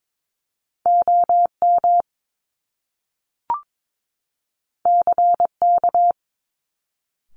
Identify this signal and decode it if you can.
Morse code, heard as a keyed tone, about 22 wpm, 700 Hz. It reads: OM CK
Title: Top Two Letter Combinations in Words 22wpm